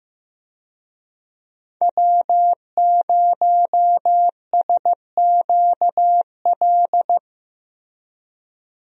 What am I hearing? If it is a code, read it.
Morse code, heard as a keyed tone, about 15 wpm, 700 Hz. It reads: W0SQL
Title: USA Call Signs - 5 Characters - Repeated 2x - 15wpm